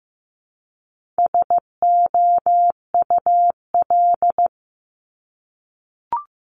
Morse code, heard as a keyed tone, about 15 wpm, 700 Hz. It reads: SOUL
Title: Multi-letter Phonemes Repeated 3x 15wpm